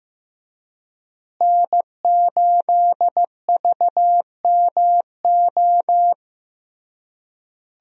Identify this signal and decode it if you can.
Morse code, heard as a keyed tone, about 15 wpm, 700 Hz. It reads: N8VMO
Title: USA Call Signs - 5 Characters - Repeated 3x - 15wpm